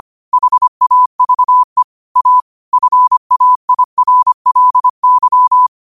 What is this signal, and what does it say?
Morse code, heard as a keyed tone, about 25 wpm, 1000 Hz. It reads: HAVE A FAIRLY
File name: qotdfiles / 25wpm